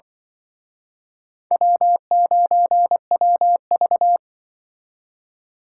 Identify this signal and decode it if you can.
Morse code, heard as a keyed tone, about 24 wpm, 700 Hz. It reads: W9WV
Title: Speed Racing - USA 1x2 Call Signs 20wpm